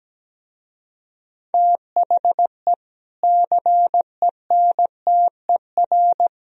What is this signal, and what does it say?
Morse code, heard as a keyed tone, about 17 wpm, 700 Hz. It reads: THE CENTER
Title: Speed-Racing - Sets of 3 Words 17wpm